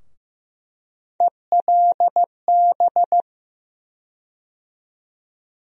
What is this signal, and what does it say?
Morse code, heard as a keyed tone, about 15 wpm, 700 Hz. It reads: ELB
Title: USA Call Sign Postfixes - 3 Letters 15wpm